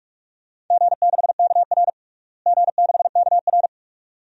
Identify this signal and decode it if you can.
Morse code, heard as a keyed tone, about 47 wpm, 700 Hz. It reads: G6KR G6KR
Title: DXCC-100 Call Signs - 4 Characters - Repeated 2x - 45wpm